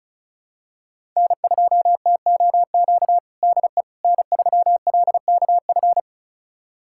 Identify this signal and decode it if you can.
Morse code, heard as a keyed tone, about 36 wpm, 700 Hz. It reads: N2TOQ DE N3LKF
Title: USA Callsign DE Callsign - 5 Characters - 35wpm